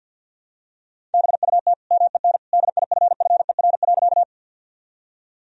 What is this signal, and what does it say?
Morse code, heard as a keyed tone, about 52 wpm, 700 Hz. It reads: BUT MEN DIFFER.
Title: Sentences from Top 1000 Words - Review Part 1 - 50wpm